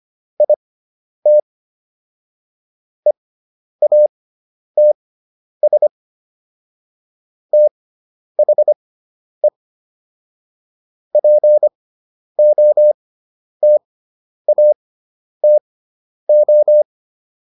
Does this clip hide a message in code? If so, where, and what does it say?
Yes: Morse code from 0.4 to 16.8 s, heard as a keyed tone, about 25 wpm, spaced as for 10 wpm, 600 Hz. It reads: IT EATS THE POTATO